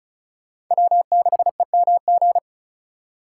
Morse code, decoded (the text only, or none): W6EMG